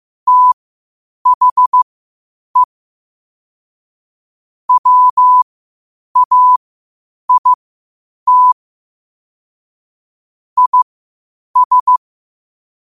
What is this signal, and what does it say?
Morse code, heard as a keyed tone, about 15 wpm, spaced as for 8 wpm, 1000 Hz. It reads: THE WAIT IS
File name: qotdfiles / 8wpm